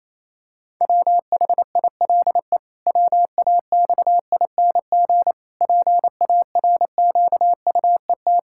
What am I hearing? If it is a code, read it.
Morse code, heard as a keyed tone, about 27 wpm, 700 Hz. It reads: WHILE WAXING PARQUET
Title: Pangram Challenge v2 - 28wpm